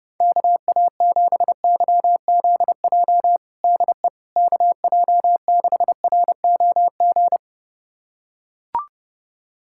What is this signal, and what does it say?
Morse code, heard as a keyed tone, about 30 wpm, 700 Hz. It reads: KA7YZJ DE KJ6ROG
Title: USA Callsign DE Callsign - 6 Characters - 30wpm